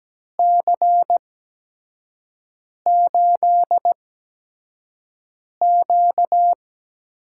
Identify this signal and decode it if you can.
Morse code, heard as a keyed tone, about 17 wpm, spaced as for 5 wpm, 700 Hz. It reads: C8Q
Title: USA Call Signs - 8x Character Spacing - 17wpm